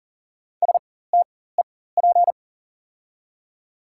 Morse code, heard as a keyed tone, about 40 wpm, spaced as for 19 wpm, 700 Hz. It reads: STEP
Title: 4x Character Spacing - Top 500 Words - 40wpm